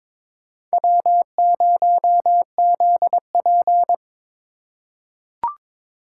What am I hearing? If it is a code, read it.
Morse code, heard as a keyed tone, about 22 wpm, 700 Hz. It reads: W0ZP
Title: K1USN SST QSO Practice - 22wpm